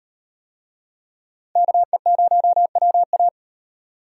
Morse code, heard as a keyed tone, about 36 wpm, 700 Hz. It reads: KE0WA